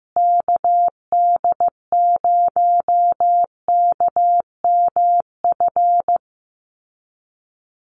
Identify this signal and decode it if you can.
Morse code, heard as a keyed tone, about 15 wpm, 700 Hz. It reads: KD0KMF
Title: USA Call Signs Repeated 2x 15wpm